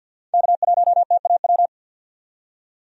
Morse code, heard as a keyed tone, about 52 wpm, 700 Hz. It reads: K1TAW
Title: USA Call Signs - 5 Characters - 50wpm